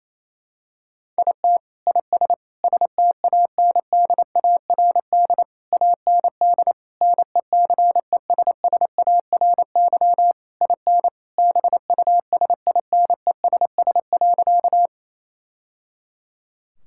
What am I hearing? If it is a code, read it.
Morse code, heard as a keyed tone, about 28 wpm, 700 Hz. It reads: IT IS STANDARD AND NECESSARY IN BUSINESS.
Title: Sentences from Top 1000 Words - Review Part 6 - 28wpm